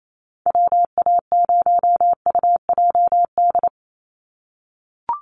Morse code, heard as a keyed tone, about 28 wpm, 700 Hz. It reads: WA0UJD
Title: USA Call Signs 28wpm